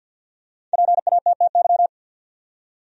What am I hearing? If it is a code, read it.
Morse code, heard as a keyed tone, about 51 wpm, 700 Hz. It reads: PATTY
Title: USA Common Names - Max 5 Letters - 50wpm